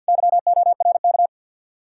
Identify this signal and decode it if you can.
Morse code, heard as a keyed tone, about 67 wpm, spaced as for 58 wpm, 700 Hz. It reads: YORK
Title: Speed-Racing Common QSO Phrases Encore - 50wpm